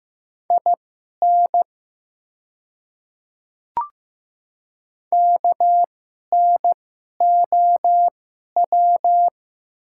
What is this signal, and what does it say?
Morse code, heard as a keyed tone, about 15 wpm, spaced as for 11 wpm, 700 Hz. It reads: IN KNOW
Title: 2x Character Spacing - Top 100 Words - 15wpm